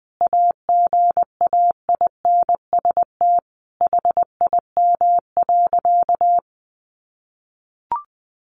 Morse code, heard as a keyed tone, about 20 wpm, 700 Hz. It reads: AGAINST HIM.
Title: Sentences from Top 1300 Words 20wpm